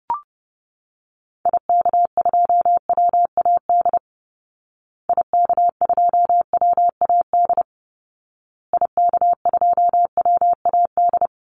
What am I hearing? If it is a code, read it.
Morse code, heard as a keyed tone, about 30 wpm, 700 Hz. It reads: IK2WAD IK2WAD IK2WAD
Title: Call Signs Repeated 2x - 30wpm